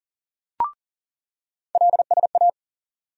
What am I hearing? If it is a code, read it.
Morse code, heard as a keyed tone, about 40 wpm, 700 Hz. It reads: LSA
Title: USA Call Sign Postfixes - 3 Letters 40wpm